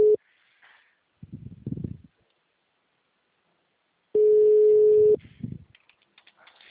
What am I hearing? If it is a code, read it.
Ringback tone (Europe, 425 Hz).